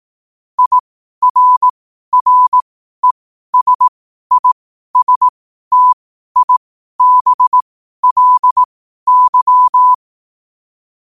Morse code, heard as a keyed tone, about 18 wpm, spaced as for 12 wpm, 1000 Hz. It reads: IRRESISTIBLY
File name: qotdfiles / 12wpm